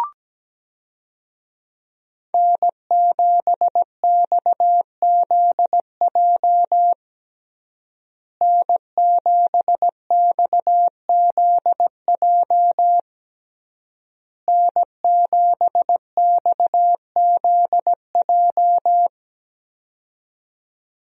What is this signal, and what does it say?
Morse code, heard as a keyed tone, about 17 wpm, 700 Hz. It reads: N7XZJ N7XZJ N7XZJ
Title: USA Call Signs - 5 Characters - Repeated 3x - 17wpm